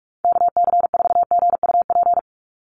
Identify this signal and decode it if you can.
Morse code, heard as a keyed tone, about 44 wpm, 700 Hz. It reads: KC4GUP
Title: USA Call Signs 45wpm